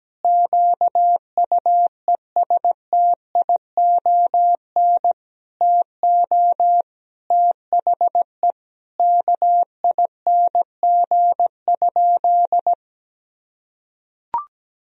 Morse code, heard as a keyed tone, about 17 wpm, 700 Hz. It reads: QUESTION TO THE KING?
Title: Sentences from Top 1000 Words - Review Part 2 - 17wpm